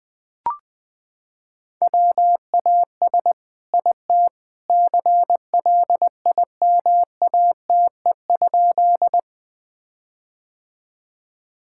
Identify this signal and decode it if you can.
Morse code, heard as a keyed tone, about 20 wpm, 700 Hz. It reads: WAS IT CLIMATE?
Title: Sentences from Top 1600 Words 20wpm